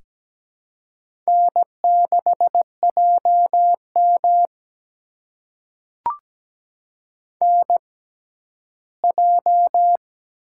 Morse code, heard as a keyed tone, about 17 wpm, spaced as for 11 wpm, 700 Hz. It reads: N6JM NJ
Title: USA Call Signs - 4 Characters - 6x Character Spacing - 17wpm